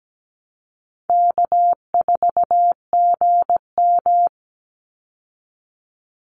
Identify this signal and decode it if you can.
Morse code, heard as a keyed tone, about 17 wpm, 700 Hz. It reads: K4GM
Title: Easy Call Signs Repeated 3x - 17wpm